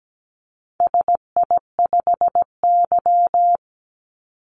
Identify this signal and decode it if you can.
Morse code, heard as a keyed tone, about 17 wpm, 700 Hz. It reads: SI5Y
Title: Call Signs - 17wpm